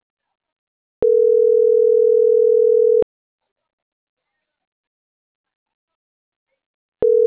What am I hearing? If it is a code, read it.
Ringback tone (North America), 440 + 480 Hz.